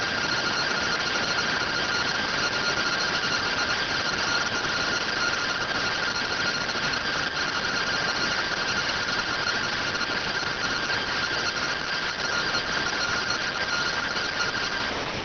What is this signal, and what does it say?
Two-tone teleprinter signal, about 170 Hz shift.